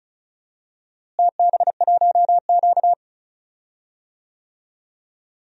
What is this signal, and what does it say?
Morse code, heard as a keyed tone, about 36 wpm, 700 Hz. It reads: TB1Q
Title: DXCC-100 Call Signs - 4 Characters - Repeated 3x - 35wpm